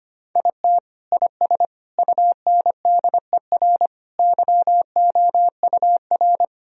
Morse code, heard as a keyed tone, about 25 wpm, 700 Hz. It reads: IT IS UNDER YOUR
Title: Sentences from Top 1000 Words - Review Part 1 - 25wpm